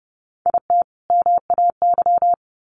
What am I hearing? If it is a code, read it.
Morse code, heard as a keyed tone, about 30 wpm, 700 Hz. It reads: IT MAY
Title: Sentences from Top 1100 Words 30wpm